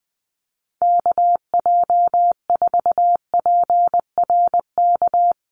Morse code, heard as a keyed tone, about 19 wpm, 700 Hz. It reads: KJ4PRK
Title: USA Call Signs Repeated 3x 20wpm